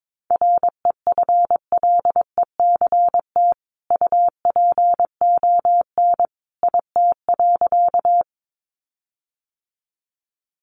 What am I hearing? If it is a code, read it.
Morse code, heard as a keyed tone, about 22 wpm, 700 Hz. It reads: REFLECT UPON IT.
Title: Sentences from Top 1500 Words 22wpm